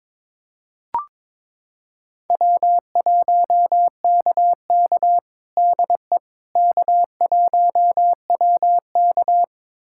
Morse code, heard as a keyed tone, about 23 wpm, 700 Hz. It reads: W1KK DE K1WK
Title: USA Callsign DE Callsign - 4 Characters - 22wpm